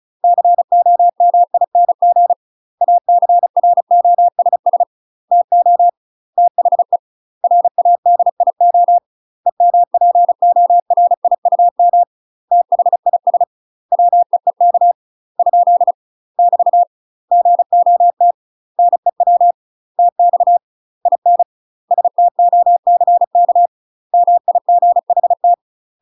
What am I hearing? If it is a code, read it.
Morse code, heard as a keyed tone, about 36 wpm, 700 Hz. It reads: COMING ACROSS TO THE RADIO EMPORIUM THIS WEEK ? <BT> GOT NEW TX IN STOCK MIGHT
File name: UK MASTER QSOs TRAINING / 35WPM / UK MASTER QSOs 0018